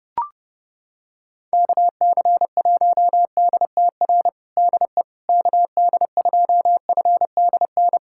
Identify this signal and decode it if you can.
Morse code, heard as a keyed tone, about 30 wpm, 700 Hz. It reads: KC1DTR DE KD2FDN
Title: USA Callsign DE Callsign 30wpm